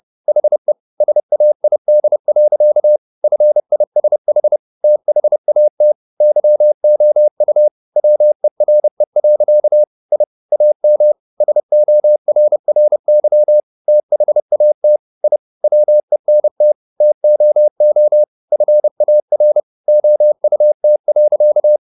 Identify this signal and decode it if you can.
Morse code, heard as a keyed tone, about 30 wpm, 600 Hz. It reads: HE SAID. FISH THAT YOU WERE. I AM SORRY THAT I WENT TOO FAR OUT.